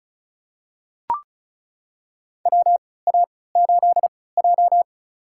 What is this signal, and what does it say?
Morse code, heard as a keyed tone, about 36 wpm, spaced as for 20 wpm, 700 Hz. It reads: WA8J
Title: USA Call Signs - 3x Character Spacing - 35wpm